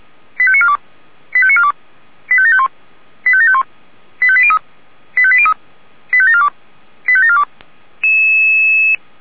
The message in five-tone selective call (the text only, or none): 85942 85942 85751 85751 85803 85803 85742 85742